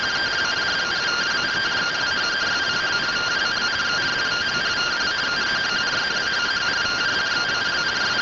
Two-tone teleprinter signal, about 170 Hz shift.